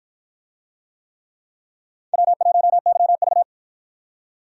Morse code, heard as a keyed tone, about 53 wpm, 700 Hz. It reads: W1YV